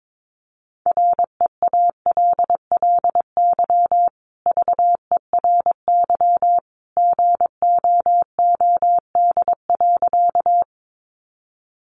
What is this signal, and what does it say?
Morse code, heard as a keyed tone, about 22 wpm, 700 Hz. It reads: REALLY VERY GOOD.
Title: Sentences from Top 600 Words 22wpm